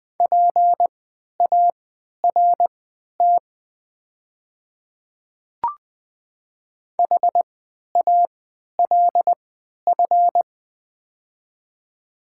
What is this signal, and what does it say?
Morse code, heard as a keyed tone, about 20 wpm, spaced as for 11 wpm, 700 Hz. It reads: PART HALF
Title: 3x Character Spacing - Top 500 Words - 20wpm